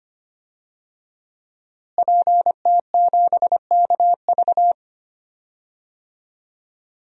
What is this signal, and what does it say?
Morse code, heard as a keyed tone, about 25 wpm, 700 Hz. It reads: PT7KV
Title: DXCC-100 Call Signs - 5 Characters - Repeated 3x - 25wpm